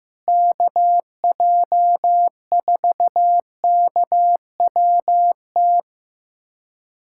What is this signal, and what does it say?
Morse code, heard as a keyed tone, about 16 wpm, 700 Hz. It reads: KJ4KWT
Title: USA Callsign DE Callsign 15wpm